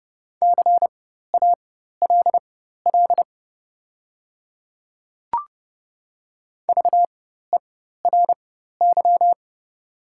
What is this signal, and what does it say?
Morse code, heard as a keyed tone, about 30 wpm, spaced as for 14 wpm, 690 Hz. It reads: CALL VERY
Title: 4x Character Spacing - Top 200 Words - 30wpm